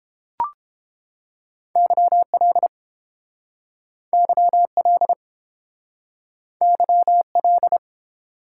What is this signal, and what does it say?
Morse code, heard as a keyed tone, about 29 wpm, 700 Hz. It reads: YL YL YL